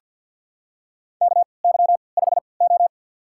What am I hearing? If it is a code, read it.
Morse code, heard as a keyed tone, about 47 wpm, spaced as for 28 wpm, 700 Hz. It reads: KY5O
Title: USA Call Signs - 4 Characters - 3x Character Spacing - 50wpm